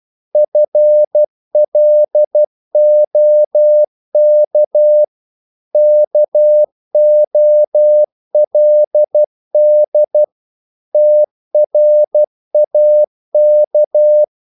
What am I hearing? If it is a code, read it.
Morse code, heard as a keyed tone, about 12 wpm, 600 Hz. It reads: FLOK KOLD TRAK
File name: Korte ord DK 12wpm